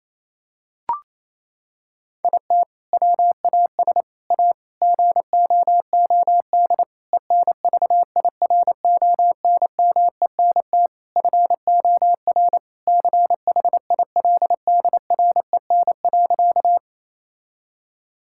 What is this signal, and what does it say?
Morse code, heard as a keyed tone, about 27 wpm, 700 Hz. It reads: IT WAS A GOOD ENVIRONMENT FOR CHILDREN.
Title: Sentences from Top 1000 Words - Review Part 5 - 28wpm